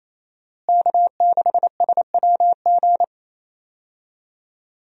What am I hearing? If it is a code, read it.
Morse code, heard as a keyed tone, about 28 wpm, 700 Hz. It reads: K6SWG
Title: USA Call Signs - 5 Characters - Repeated 3x - 28wpm